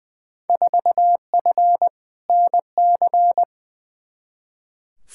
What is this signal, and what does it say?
Morse code, heard as a keyed tone, about 20 wpm, 700 Hz. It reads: 4F NC
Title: Field Day Exchanges - 20wpm